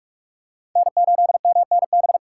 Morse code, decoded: N8MNB

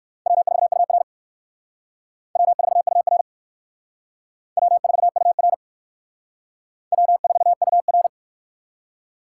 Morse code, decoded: W4UR W4UR W4UR W4UR

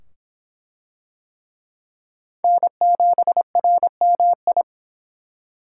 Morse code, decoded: N7RMI

